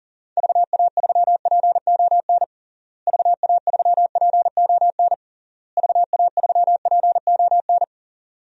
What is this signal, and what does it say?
Morse code, heard as a keyed tone, about 40 wpm, 700 Hz. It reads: VA3PON VA3PON VA3PON
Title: DXCC-100 Call Signs - 6 Characters - Repeated 3x - 40wpm